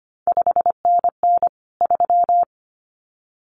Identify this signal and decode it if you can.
Morse code, heard as a keyed tone, about 25 wpm, 700 Hz. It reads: 5NN 3
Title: CQ WW DX Contest Exchange Encore - 25wpm